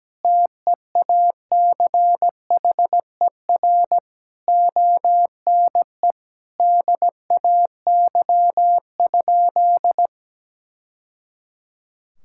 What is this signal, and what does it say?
Morse code, heard as a keyed tone, about 17 wpm, 700 Hz. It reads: TEACHER ONE DAY?
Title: Sentences from Top 1000 Words - Review Part 6 - 17wpm